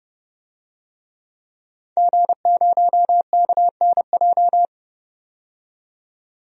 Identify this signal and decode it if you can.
Morse code, heard as a keyed tone, about 30 wpm, 700 Hz. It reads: G0KNJ
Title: DXCC-100 Call Signs - 5 Characters - 30wpm